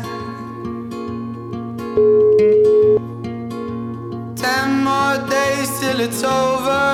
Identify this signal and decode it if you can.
Ringback tone (Europe, 425 Hz), with noise.